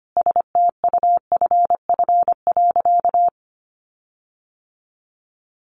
Morse code, heard as a keyed tone, about 25 wpm, 700 Hz. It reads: STUFF.
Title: Sentences from Top 1400 Words 25wpm